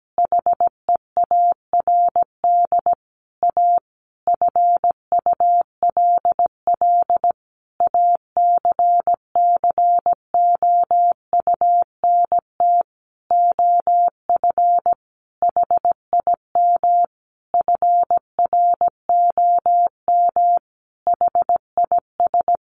Morse code, heard as a keyed tone, about 17 wpm, 700 Hz. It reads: HEARD A FULL ACCOUNT OF HIM FROM HIS
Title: Sentences from Top 900 Words 17wpm